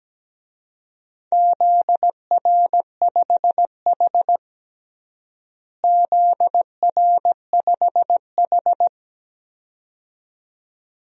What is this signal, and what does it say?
Morse code, heard as a keyed tone, about 17 wpm, 700 Hz. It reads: ZR5H ZR5H
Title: DXCC-100 Call Signs - 4 Characters - Repeated 2x - 17wpm